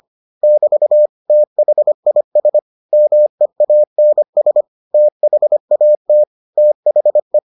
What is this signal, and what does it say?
Morse code, heard as a keyed tone, about 25 wpm, 600 Hz. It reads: <BT> THIS MEANS THAT THE